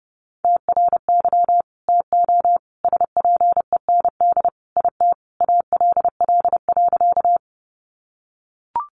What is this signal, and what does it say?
Morse code, heard as a keyed tone, about 30 wpm, 700 Hz. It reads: TRY TO SPEND IT ALL.
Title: Sentences from Top 800 Words 30wpm